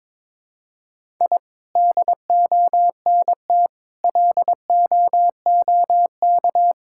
Morse code, heard as a keyed tone, about 22 wpm, 700 Hz. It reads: I DONT LOOK
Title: Sentences from Top 1000 Words - Review Part 5 - 22wpm